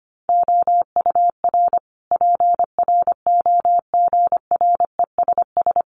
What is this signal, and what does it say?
Morse code, heard as a keyed tone, about 25 wpm, 700 Hz. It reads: OUR PROGRESS
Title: Sentences from Top 1400 Words 25wpm